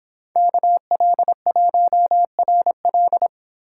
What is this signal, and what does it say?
Morse code, heard as a keyed tone, about 26 wpm, 700 Hz. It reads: KL1RL